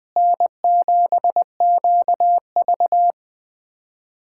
Morse code, heard as a keyed tone, about 20 wpm, 700 Hz. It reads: N7QV
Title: USA Callsign DE Callsign - 4 Characters - 20wpm